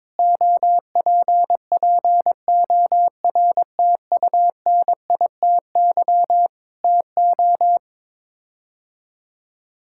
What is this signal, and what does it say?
Morse code, heard as a keyed tone, about 22 wpm, 700 Hz. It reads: OPPORTUNITY TO
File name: practice / 22wpm / Sets of 8 Words - 22wpm